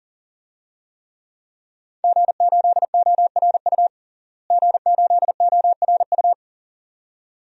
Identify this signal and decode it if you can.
Morse code, heard as a keyed tone, about 40 wpm, 700 Hz. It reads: G8ORU G8ORU
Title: DXCC-100 Call Signs - 5 Characters - Repeated 2x - 40wpm